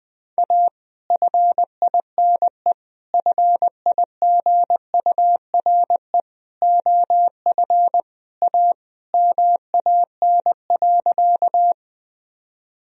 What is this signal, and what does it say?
Morse code, heard as a keyed tone, about 20 wpm, 700 Hz. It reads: A FINE FIGURE OF A MAN.
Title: Sentences from Top 1000 Words - Review Part 2 - 20wpm